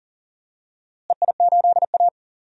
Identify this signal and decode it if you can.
Morse code, heard as a keyed tone, about 40 wpm, 700 Hz. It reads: EI8A
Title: DXCC-100 Call Signs - 4 Characters - Repeated 2x - 40wpm